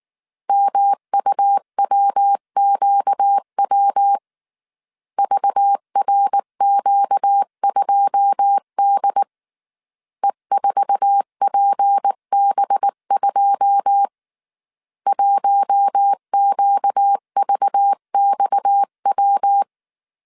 Morse code, heard as a keyed tone, about 19 wpm, 800 Hz. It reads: MUWQW VRQ2D E4PB2 1QVXW
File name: CW / DET / 1 9 1500C S B 18wpm 2551